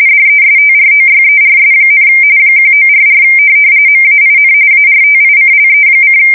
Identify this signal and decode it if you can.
RTTY, about 45.45 baud. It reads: DESTROYERS, 1944 MAIN ARTICLE: